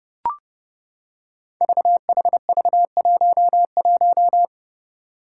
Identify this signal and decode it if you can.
Morse code, heard as a keyed tone, about 30 wpm, 700 Hz. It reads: VHV11
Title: Five Challenging Letter-Number Combinations 30wpm